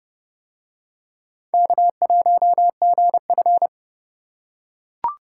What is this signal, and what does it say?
Morse code, heard as a keyed tone, about 29 wpm, 700 Hz. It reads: K1GF